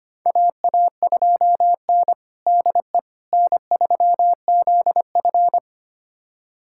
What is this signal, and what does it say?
Morse code, heard as a keyed tone, about 25 wpm, 700 Hz. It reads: AA2N DE N3ZF
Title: USA Callsign DE Callsign - 4 Characters - 25wpm